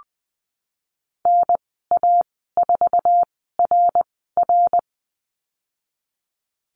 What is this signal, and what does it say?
Morse code, heard as a keyed tone, about 20 wpm, spaced as for 15 wpm, 700 Hz. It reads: NA4RR
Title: 088 Lesson 15g - Number 5 - Call Signs 20wpm